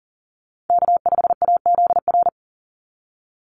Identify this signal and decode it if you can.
Morse code, heard as a keyed tone, about 40 wpm, 700 Hz. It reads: K5AZR
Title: Call Signs Repeated 3x - 40wpm